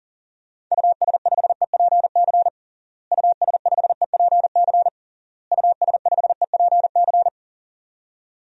Morse code, heard as a keyed tone, about 40 wpm, 700 Hz. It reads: US5EPC US5EPC US5EPC
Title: DXCC-100 Call Signs - 6 Characters - Repeated 3x - 40wpm